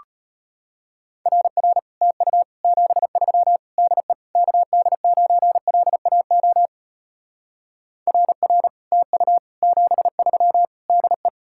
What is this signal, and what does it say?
Morse code, heard as a keyed tone, about 33 wpm, 700 Hz. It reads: RR TU 73 DE KD9LAO RR TU 73 DE